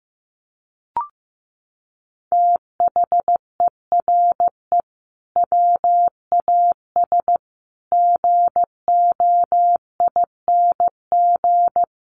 Morse code, heard as a keyed tone, about 15 wpm, 700 Hz. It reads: THERE WAS GOING
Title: Sets of 5 Words Encore2 15wpm